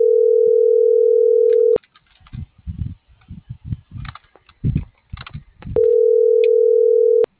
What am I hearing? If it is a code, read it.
Ringback tone (North America), 440 + 480 Hz.